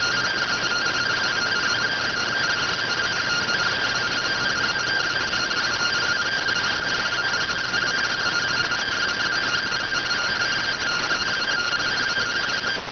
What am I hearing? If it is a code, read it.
Two-tone teleprinter signal, about 170 Hz shift.